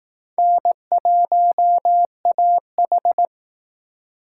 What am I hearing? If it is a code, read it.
Morse code, heard as a keyed tone, about 18 wpm, 700 Hz. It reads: N1AH
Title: Speed Racing - USA 1x2 Call Signs 15wpm